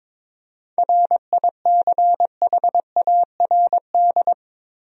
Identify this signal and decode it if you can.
Morse code, heard as a keyed tone, about 22 wpm, 700 Hz. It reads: RICHARD
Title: USA Common Names - 22wpm